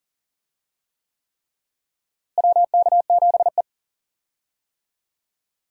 Morse code, heard as a keyed tone, about 40 wpm, 700 Hz. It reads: WK7E
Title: USA Call Signs - 4 Characters - Repeated 3x - 40wpm